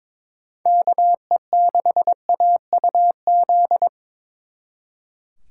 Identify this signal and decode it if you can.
Morse code, heard as a keyed tone, about 25 wpm, 700 Hz. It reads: KE6AUZ
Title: Speed Racing - USA 2x3 Call Signs 22wpm